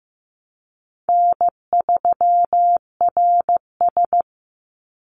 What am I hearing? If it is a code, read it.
Morse code, heard as a keyed tone, about 15 wpm, 700 Hz. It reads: N3RS
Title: Easy Call Signs - 15wpm